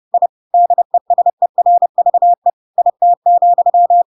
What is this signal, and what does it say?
Morse code, heard as a keyed tone, about 30 wpm, 700 Hz. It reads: I DESERVE IT,
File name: Pinocchio / Pinocchio ch3 30 30 ik1hge